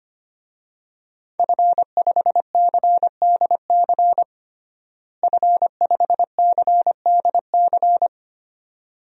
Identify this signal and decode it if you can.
Morse code, heard as a keyed tone, about 25 wpm, 700 Hz. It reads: F5CDC F5CDC
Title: DXCC-100 Call Signs - 5 Characters - Repeated 2x - 25wpm